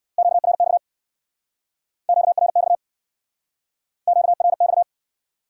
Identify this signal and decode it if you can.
Morse code, heard as a keyed tone, about 66 wpm, 700 Hz. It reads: QRX QRX QRX